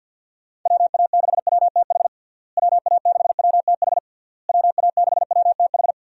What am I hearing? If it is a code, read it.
Morse code, heard as a keyed tone, about 52 wpm, 700 Hz. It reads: WA6WTH WA6WTH WA6WTH
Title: USA Call Signs - 6 Characters - Repeated 3x - 50wpm